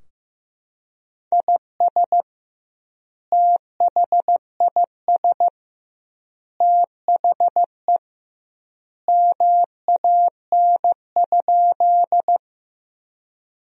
Morse code, heard as a keyed tone, about 16 wpm, 700 Hz. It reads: IS THIS THE MAN?
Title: 2x Word-Spacing - Sentences from Top 200 Words - 15wpm